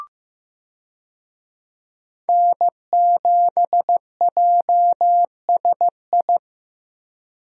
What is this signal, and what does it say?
Morse code, heard as a keyed tone, about 15 wpm, 700 Hz. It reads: N7JSI
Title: USA Call Signs - 5 Characters - Repeated 2x - 15wpm